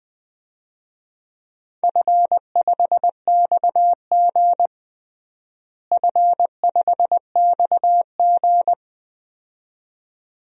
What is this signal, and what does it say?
Morse code, heard as a keyed tone, about 20 wpm, 700 Hz. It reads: F5XG F5XG
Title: DXCC-100 Call Signs - 4 Characters - Repeated 2x - 20wpm